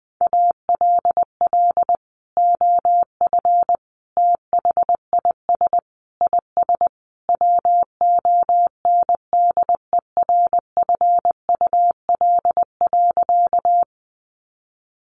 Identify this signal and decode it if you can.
Morse code, heard as a keyed tone, about 19 wpm, 700 Hz. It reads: ALL OF THIS IS WONDERFUL.
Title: Sentences from Top 1100 Words 20wpm